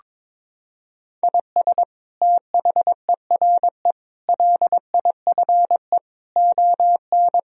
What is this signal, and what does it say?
Morse code, heard as a keyed tone, about 22 wpm, 700 Hz. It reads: IS THERE LIFE ON
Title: Sentences from Top 1000 Words - Review Part 3 - 22wpm